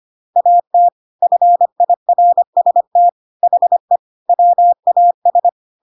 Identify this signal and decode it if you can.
Morse code, heard as a keyed tone, about 25 wpm, 700 Hz. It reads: AT FIRST HE WAS
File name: Pinocchio / Pinocchio ch7 25 25 ik1hge